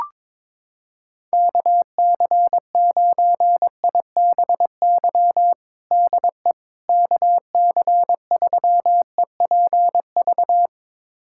Morse code, heard as a keyed tone, about 22 wpm, 700 Hz. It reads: KC9IBY DE KC3EPV
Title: USA Callsign DE Callsign - 6 Characters - 22wpm